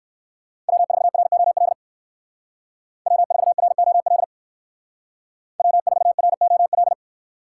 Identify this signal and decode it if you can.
Morse code, heard as a keyed tone, about 63 wpm, 700 Hz. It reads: W4ROL W4ROL W4ROL